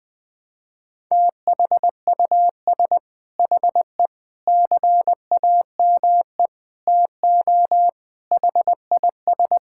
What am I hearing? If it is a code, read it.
Morse code, heard as a keyed tone, about 21 wpm, 700 Hz. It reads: THUS HE CAME TO HIS
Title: Sentences from Top 1000 Words - Review Part 5 - 20wpm